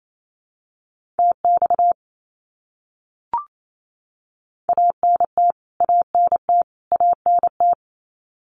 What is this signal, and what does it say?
Morse code, heard as a keyed tone, about 28 wpm, 700 Hz. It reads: TX ANT ANT ANT
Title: Abbreviations Repeated 3x - 28wpm